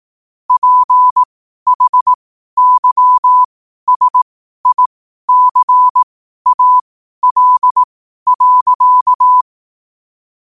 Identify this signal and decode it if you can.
Morse code, heard as a keyed tone, about 18 wpm, spaced as for 12 wpm, 1000 Hz. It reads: PHYSICAL.
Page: Quotes for Thu, 14 Aug 2025 in Morse Code at 12 words per minute.